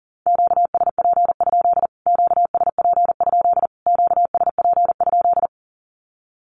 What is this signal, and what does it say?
Morse code, heard as a keyed tone, about 40 wpm, 700 Hz. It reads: QSP? QSP? QSP?
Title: Q-Codes Repeated 3x 40wpm